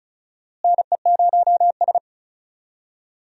Morse code, heard as a keyed tone, about 34 wpm, 700 Hz. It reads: NE0S
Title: USA Call Signs - 4 Characters - Repeated 2x - 35wpm